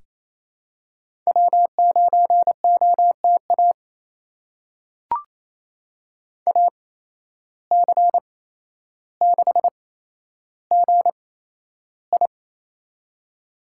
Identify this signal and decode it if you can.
Morse code, heard as a keyed tone, about 28 wpm, spaced as for 12 wpm, 700 Hz. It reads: W9OTA AC6GI
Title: USA Call Signs - 5 Characters - 8x Character Spacing - 28wpm